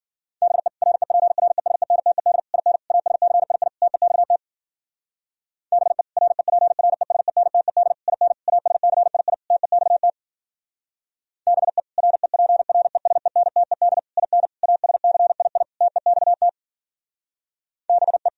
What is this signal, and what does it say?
Morse code, heard as a keyed tone, about 51 wpm, 700 Hz. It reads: BE REPRESENTED IN ASCII TEXT BE REPRESENTED IN ASCII TEXT BE REPRESENTED IN ASCII TEXT BE